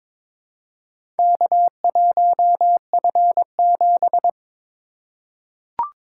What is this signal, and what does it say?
Morse code, heard as a keyed tone, about 22 wpm, 700 Hz. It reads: K1F7
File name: practice / 22wpm / Four Letter-Number Combinations - 22wpm